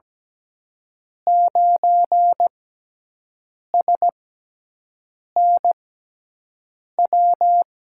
Morse code, heard as a keyed tone, about 17 wpm, spaced as for 6 wpm, 700 Hz. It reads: 9SNW
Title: USA Call Signs - 5 Characters - 6x Character Spacing - 17wpm